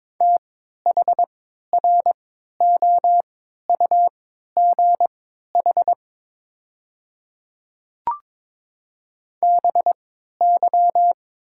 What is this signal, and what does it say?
Morse code, heard as a keyed tone, about 22 wpm, spaced as for 12 wpm, 690 Hz. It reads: THROUGH BY
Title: 3x Character Spacing - Top 300 Words - 22wpm